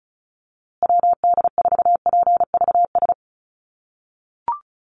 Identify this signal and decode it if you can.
Morse code, heard as a keyed tone, about 34 wpm, 700 Hz. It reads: WD4PVS
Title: USA Call Signs Repeated 2x 35wpm